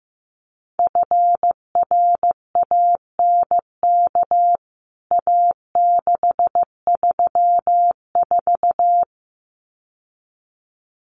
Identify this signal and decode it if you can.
Morse code, heard as a keyed tone, about 15 wpm, 700 Hz. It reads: FRANK A634
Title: CWT Exchange 15wpm